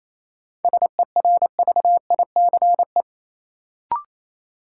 Morse code, heard as a keyed tone, about 28 wpm, 700 Hz. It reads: SERVICE
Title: Sentences from Top 700 Words - New Word Drill - 28wpm